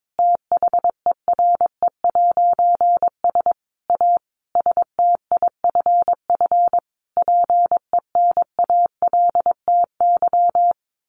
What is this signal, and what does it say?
Morse code, heard as a keyed tone, about 22 wpm, 700 Hz. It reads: THERE'S A STIFF PENALTY
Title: A Sound Of Thunder 22wpm